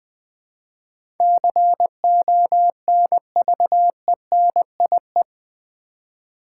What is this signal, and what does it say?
Morse code, heard as a keyed tone, about 20 wpm, 700 Hz. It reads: CONVENIE
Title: Build A Word - 4 Letters At A Time - 11-Letter Words - 20wpm